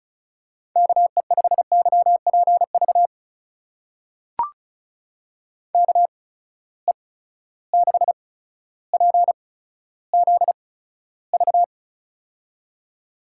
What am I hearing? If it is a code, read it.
Morse code, heard as a keyed tone, about 36 wpm, spaced as for 15 wpm, 700 Hz. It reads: KE5YPV KE6PZV
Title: USA Call Signs - 6 Characters - 8x Character Spacing - 35wpm